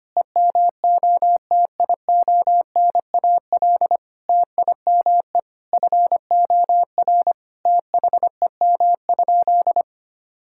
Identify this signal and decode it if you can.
Morse code, heard as a keyed tone, about 25 wpm, 700 Hz. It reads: EMOTIONAL TIME FOR THEM?
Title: Sentences from Top 1000 Words - Review Part 5 - 25wpm